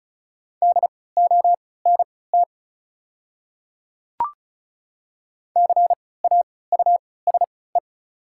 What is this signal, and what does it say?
Morse code, heard as a keyed tone, about 36 wpm, spaced as for 20 wpm, 700 Hz. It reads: DONT CAUSE
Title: 3x Character Spacing - Top 300 Words - 35wpm